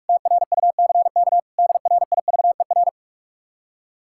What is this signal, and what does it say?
Morse code, heard as a keyed tone, about 59 wpm, 700 Hz. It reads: TRUCK DRIVER